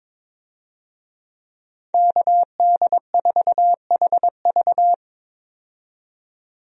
Morse code, heard as a keyed tone, about 22 wpm, 700 Hz. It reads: KD4HV
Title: USA Call Signs - 5 Characters - 22wpm